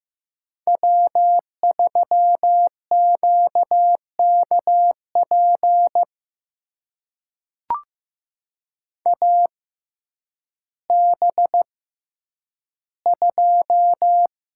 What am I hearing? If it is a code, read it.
Morse code, heard as a keyed tone, about 15 wpm, spaced as for 9 wpm, 700 Hz. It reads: W3QKP AB2
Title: USA Call Signs - 5 Characters - 6x Character Spacing - 15wpm